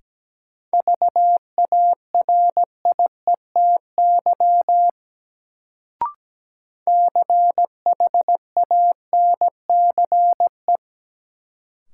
Morse code, heard as a keyed tone, about 17 wpm, 700 Hz. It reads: VARIETY CHANCE
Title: Sets of 3 Words - New Word Drill - 17wpm